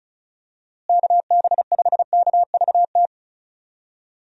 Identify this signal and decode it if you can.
Morse code, heard as a keyed tone, about 34 wpm, 700 Hz. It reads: KB5KVT
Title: USA Call Signs - 6 Characters - Repeated 3x - 35wpm